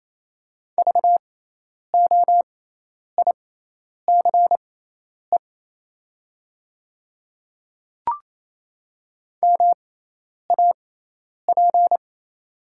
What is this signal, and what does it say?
Morse code, heard as a keyed tone, about 28 wpm, spaced as for 10 wpm, 700 Hz. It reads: VOICE MAP
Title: 6x Character Spacing - Top 500 Words - 28wpm